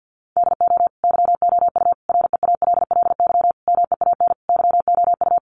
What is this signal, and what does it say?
Morse code, heard as a keyed tone, about 50 wpm, 700 Hz. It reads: DO YOU REALLY MEAN YOU
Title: Sentences from Top 600 Words 50wpm